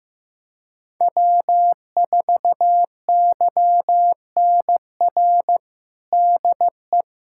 Morse code, heard as a keyed tone, about 15 wpm, 700 Hz. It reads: W4YNR DE
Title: USA Callsign DE Callsign 15wpm